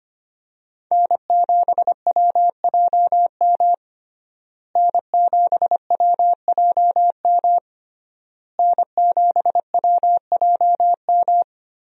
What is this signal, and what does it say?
Morse code, heard as a keyed tone, about 25 wpm, 700 Hz. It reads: N7WJM N7WJM N7WJM
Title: USA Call Signs - 5 Characters - Repeated 3x - 25wpm